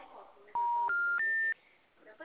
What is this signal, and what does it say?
Special information tone, with speech.
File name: out- -2011- - - .67